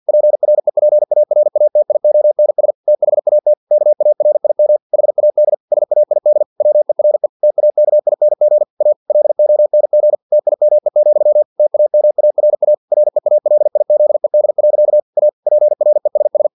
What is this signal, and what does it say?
Morse code, heard as a keyed tone, about 50 wpm, 600 Hz. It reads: PREPARATIONS THAT KARIM HAD SAID WERE TAKING A LONG TIME, TAMARA REALIZED. A PRESS
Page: Never 49wpm